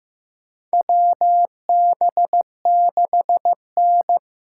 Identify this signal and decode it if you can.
Morse code, heard as a keyed tone, about 15 wpm, 700 Hz. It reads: WB6N
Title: USA Call Signs - 6 Characters - 8x Character Spacing - 15wpm